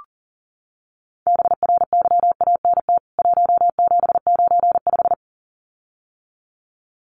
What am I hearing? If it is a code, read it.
Morse code, heard as a keyed tone, about 37 wpm, 700 Hz. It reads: BRYANT 1795
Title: CWT Exchange 40wpm